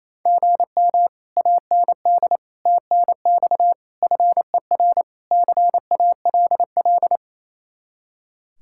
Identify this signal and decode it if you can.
Morse code, heard as a keyed tone, about 34 wpm, 700 Hz. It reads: GM AND TNX FER CALL
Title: Speed-Racing Common QSO Phrases - 28wpm